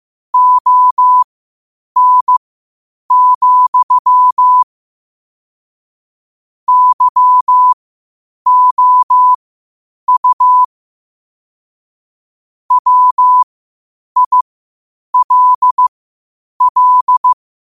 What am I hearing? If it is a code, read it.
Morse code, heard as a keyed tone, about 15 wpm, spaced as for 8 wpm, 1000 Hz. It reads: ON, YOU WILL
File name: qotdfiles / 8wpm